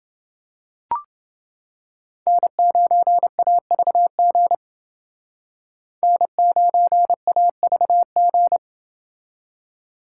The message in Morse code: N9AVG N9AVG

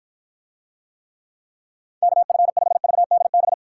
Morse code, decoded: KF5VDB

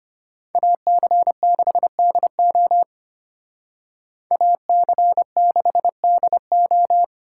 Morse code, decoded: AC6DO AC6DO

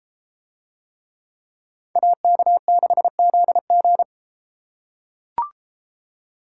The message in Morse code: AK6ZG

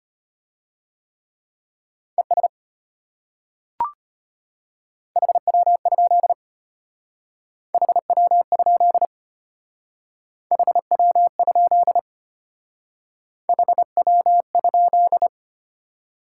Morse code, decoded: ES HW? HW? HW? HW?

